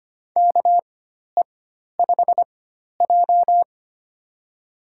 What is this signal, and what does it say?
Morse code, heard as a keyed tone, about 25 wpm, spaced as for 12 wpm, 700 Hz. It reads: KE5J
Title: USA Call Signs - 4 Characters - 4x Character Spacing - 25wpm